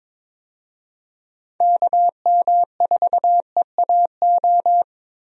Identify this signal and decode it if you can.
Morse code, heard as a keyed tone, about 22 wpm, 700 Hz. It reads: KM4EAO
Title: USA Call Signs - 6 Characters - Repeated 3x - 22wpm